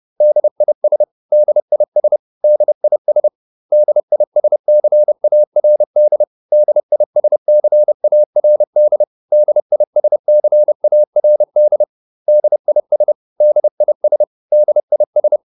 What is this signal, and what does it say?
Morse code, heard as a keyed tone, about 30 wpm, 600 Hz. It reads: DIS DIS DIS DISCARD DISCARD DISCARD DIS DIS DIS